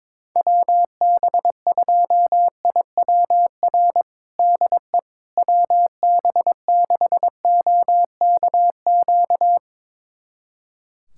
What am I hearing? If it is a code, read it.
Morse code, heard as a keyed tone, about 22 wpm, 700 Hz. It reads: WB2IWR DE WB6OKQ
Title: USA Callsign DE Callsign 22wpm